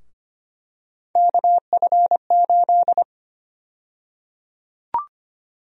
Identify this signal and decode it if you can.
Morse code, heard as a keyed tone, about 25 wpm, 700 Hz. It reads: KF8
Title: USA Call Sign Prefixes - Repeated 2x - 25wpm